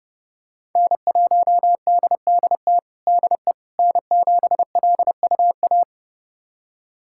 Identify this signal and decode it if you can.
Morse code, heard as a keyed tone, about 30 wpm, 700 Hz. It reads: N1DDT DE N7LUA
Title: USA Callsign DE Callsign - 5 Characters - 30wpm